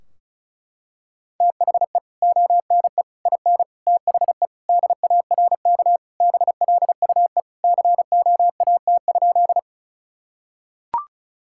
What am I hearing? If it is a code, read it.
Morse code, heard as a keyed tone, about 34 wpm, 700 Hz. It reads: THE ONE IN THE DARK BLUE COAT?
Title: Sentences from Top 1600 Words 35wpm